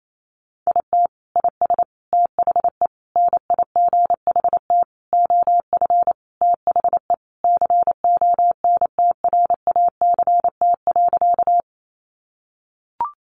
Morse code, heard as a keyed tone, about 28 wpm, 700 Hz. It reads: IT IS THE NIGHT OF THE CONTRACT.
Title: Sentences from Top 1400 Words 28wpm